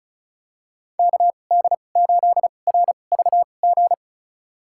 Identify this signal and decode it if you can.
Morse code, heard as a keyed tone, about 36 wpm, spaced as for 25 wpm, 700 Hz. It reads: KD8RVG
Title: USA Call Signs - 6 Characters - 2x Character Spacing - 35wpm